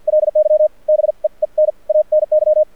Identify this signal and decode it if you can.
Morse code, heard as a keyed tone, about 50 wpm, spaced as for 41 wpm, 610 Hz. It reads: LY BEEN ANY
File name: archived / CRSnet